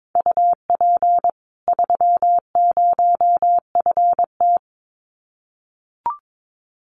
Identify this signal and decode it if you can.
Morse code, heard as a keyed tone, about 22 wpm, 700 Hz. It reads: UP 30FT
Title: Common QSO Phrases 22wpm